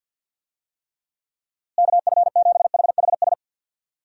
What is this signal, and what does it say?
Morse code, heard as a keyed tone, about 50 wpm, 700 Hz. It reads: KF7HHS
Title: USA Call Signs - 6 Characters - Repeated 2x - 50wpm